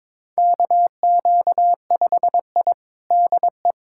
Morse code, heard as a keyed tone, about 22 wpm, 700 Hz. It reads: KQ5I DE
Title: USA Callsign DE Callsign - 4 Characters - 22wpm